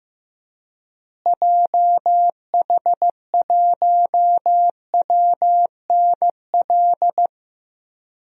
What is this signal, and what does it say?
Morse code, heard as a keyed tone, about 15 wpm, 700 Hz. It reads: JH1WNL
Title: DXCC-100 Call Signs - 6 Characters - Repeated 3x - 15wpm